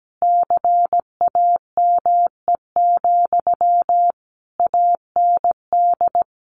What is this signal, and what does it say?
Morse code, heard as a keyed tone, about 17 wpm, 700 Hz. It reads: CAME, AND
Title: Sentences from Top 1500 Words 17wpm